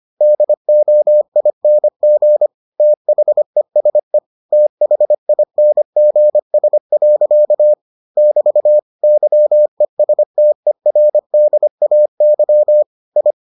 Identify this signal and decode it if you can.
Morse code, heard as a keyed tone, about 25 wpm, 600 Hz. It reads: DOING THESE THINGS. <BT> YESTERDAY I